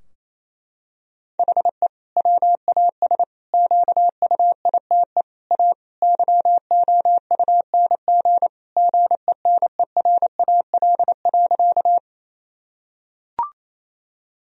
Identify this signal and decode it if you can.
Morse code, heard as a keyed tone, about 28 wpm, 700 Hz. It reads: HE WAS QUITE A YOUNG GENERAL.
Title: Sentences from Top 1000 Words - Review Part 3 - 28wpm